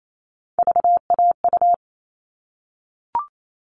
Morse code, heard as a keyed tone, about 28 wpm, 700 Hz. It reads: VAU
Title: All Cut Numbers - 3-Digit 28wpm